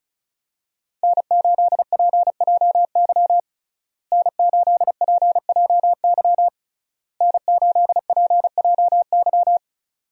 Morse code, heard as a keyed tone, about 34 wpm, 700 Hz. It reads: N8PJY N8PJY N8PJY
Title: USA Call Signs - 5 Characters - Repeated 3x - 35wpm